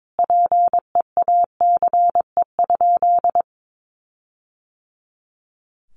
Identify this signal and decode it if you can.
Morse code, heard as a keyed tone, about 22 wpm, 700 Hz. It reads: PEACE?
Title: Sentences from Top 1500 Words 22wpm